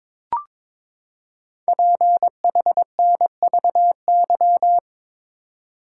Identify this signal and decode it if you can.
Morse code, heard as a keyed tone, about 23 wpm, 700 Hz. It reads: PHNVY
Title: Code Groups - 5 Letters - 22wpm